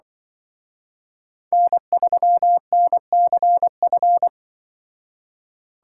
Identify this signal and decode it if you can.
Morse code, heard as a keyed tone, about 24 wpm, 700 Hz. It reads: N3NCF
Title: Speed Racing - USA 1x3 Call Signs 20wpm